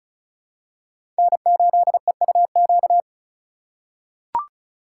Morse code, heard as a keyed tone, about 36 wpm, 700 Hz. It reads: N8EUQ
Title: USA Call Signs - 5 Characters - 2x Character Spacing - 35wpm